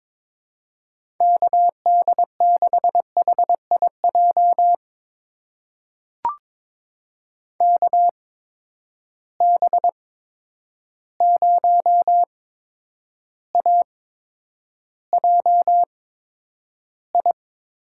Morse code, heard as a keyed tone, about 22 wpm, spaced as for 9 wpm, 700 Hz. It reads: KD6HIJ KB0AJI
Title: USA Call Signs - 6 Characters - 8x Character Spacing - 22wpm